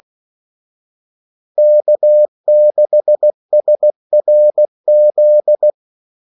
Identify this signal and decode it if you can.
Morse code, heard as a keyed tone, about 16 wpm, 600 Hz. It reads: K6SRZ
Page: Kaldesignaler 16 wpm